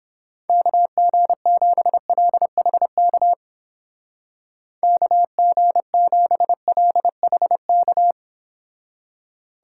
Morse code, heard as a keyed tone, about 31 wpm, 700 Hz. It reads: KG7LHK KG7LHK